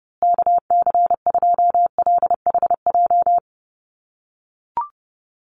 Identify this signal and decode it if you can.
Morse code, heard as a keyed tone, about 30 wpm, 700 Hz. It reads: KC2LHJ
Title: USA Call Signs Repeated 2x 30wpm